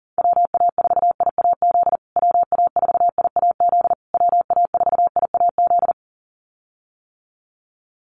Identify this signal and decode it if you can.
Morse code, heard as a keyed tone, about 40 wpm, 700 Hz. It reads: WA4IAZ WA4IAZ WA4IAZ
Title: USA Call Signs Repeated 3x 40wpm